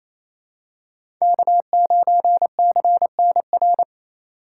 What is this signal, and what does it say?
Morse code, heard as a keyed tone, about 28 wpm, 700 Hz. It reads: K9CNR
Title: USA Call Signs - 5 Characters - 4x Character Spacing - 28wpm